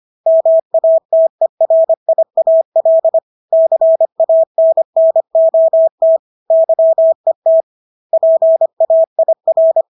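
Morse code, heard as a keyed tone, about 25 wpm, 650 Hz. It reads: MATERIAL CANNOT YET PAIR
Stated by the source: Morse practice of English words at 25 wpm from CW med Gnister.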